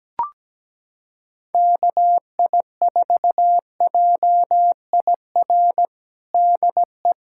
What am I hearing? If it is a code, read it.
Morse code, heard as a keyed tone, about 17 wpm, 700 Hz. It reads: KI4JIR DE
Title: USA Callsign DE Callsign - 6 Characters - 17wpm